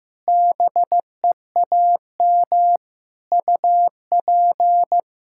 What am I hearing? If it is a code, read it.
Morse code, heard as a keyed tone, about 15 wpm, 700 Hz. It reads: BEAM UP
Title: Practice QSO - Exchange 2 - Info - 15wpm